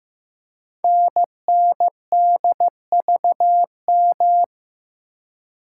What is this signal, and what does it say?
Morse code, heard as a keyed tone, about 15 wpm, 700 Hz. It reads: NNDVM
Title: Code Groups - 5 Letters - 15wpm